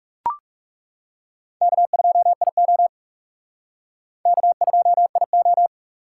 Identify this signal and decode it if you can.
Morse code, heard as a keyed tone, about 38 wpm, 700 Hz. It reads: K2IO K2IO
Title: Speed Racing - USA 1x2 Call Signs 30wpm